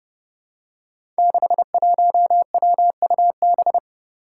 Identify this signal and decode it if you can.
Morse code, heard as a keyed tone, about 30 wpm, 700 Hz. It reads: 61WUB
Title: Five Challenging Letter-Number Combinations 30wpm